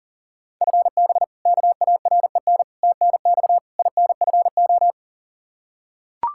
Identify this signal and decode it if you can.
Morse code, heard as a keyed tone, about 40 wpm, 700 Hz. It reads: FB KAREN TNX INFO
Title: K1USN SST QSO Practice 40wpm